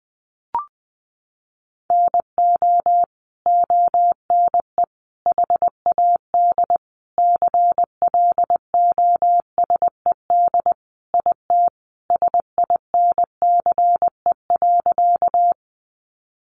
Morse code, heard as a keyed tone, about 20 wpm, 700 Hz. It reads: NO ONE HAD CLOSED IT SINCE.
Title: Sentences from Top 1400 Words 20wpm